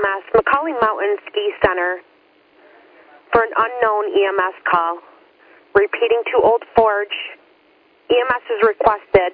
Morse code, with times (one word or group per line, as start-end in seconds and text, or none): none